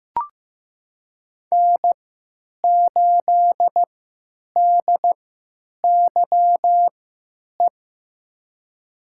Morse code, heard as a keyed tone, about 15 wpm, spaced as for 9 wpm, 700 Hz. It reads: N8DYE
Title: USA Call Signs - 5 Characters - 3x Character Spacing - 15wpm